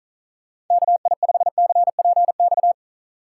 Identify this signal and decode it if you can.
Morse code, heard as a keyed tone, about 44 wpm, 700 Hz. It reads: KI5CPX